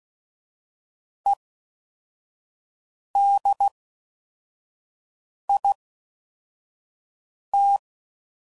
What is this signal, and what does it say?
Morse code, heard as a keyed tone, about 16 wpm, spaced as for 4 wpm, 800 Hz. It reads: EDIT